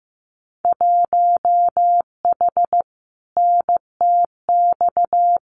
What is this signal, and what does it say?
Morse code, heard as a keyed tone, about 15 wpm, 700 Hz. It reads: 1H NTX
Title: Winter Field Day Exchanges 15wpm